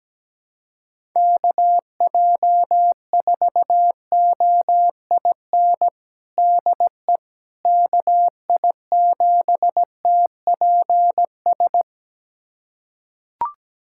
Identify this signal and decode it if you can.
Morse code, heard as a keyed tone, about 17 wpm, 700 Hz. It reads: KJ4OIN DE KI7TPS
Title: USA Callsign DE Callsign 17wpm